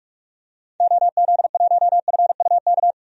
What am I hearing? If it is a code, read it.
Morse code, heard as a keyed tone, about 43 wpm, 700 Hz. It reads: OZ1FUK
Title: DXCC-100 Call Signs - 6 Characters - 45wpm